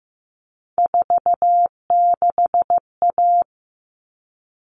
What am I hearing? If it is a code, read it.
Morse code, heard as a keyed tone, about 15 wpm, 700 Hz. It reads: 46A
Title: All Cut Numbers - 3-Digit 15wpm